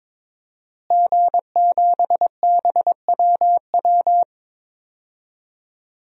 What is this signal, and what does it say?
Morse code, heard as a keyed tone, about 22 wpm, 700 Hz. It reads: G7BWW
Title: DXCC-100 Call Signs - 5 Characters - Repeated 2x - 22wpm